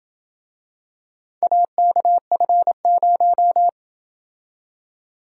Morse code, heard as a keyed tone, about 27 wpm, 700 Hz. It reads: AKF0